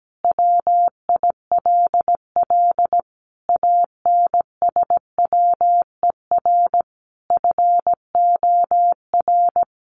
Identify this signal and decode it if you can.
Morse code, heard as a keyed tone, about 17 wpm, 700 Hz. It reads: WILL ANSWER FOR
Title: Sentences from Top 300 Words 17wpm